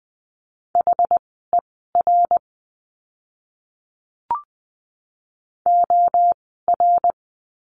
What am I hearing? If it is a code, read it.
Morse code, heard as a keyed tone, about 20 wpm, spaced as for 14 wpm, 700 Hz. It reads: HER OR
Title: 061 Lesson 11f - Letter h - Words 20wpm